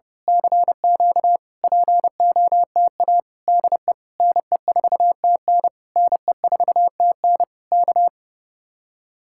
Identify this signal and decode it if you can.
Morse code, heard as a keyed tone, about 28 wpm, 700 Hz. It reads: CQ POTA DE NE4TN NE4TN K
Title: Speed-Racing Common QSO Phrases - 30wpm